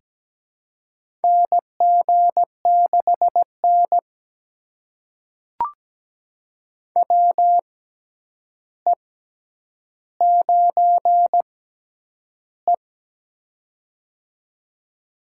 Morse code, heard as a keyed tone, about 17 wpm, spaced as for 9 wpm, 700 Hz. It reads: NG6N WE9E
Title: USA Call Signs - 4 Characters - 6x Character Spacing - 17wpm